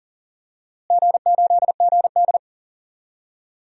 Morse code, heard as a keyed tone, about 40 wpm, 700 Hz. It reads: G8GD
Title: DXCC-100 Call Signs - 4 Characters - Repeated 3x - 40wpm